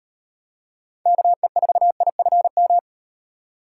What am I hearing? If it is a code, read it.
Morse code, heard as a keyed tone, about 38 wpm, 700 Hz. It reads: KE4IFM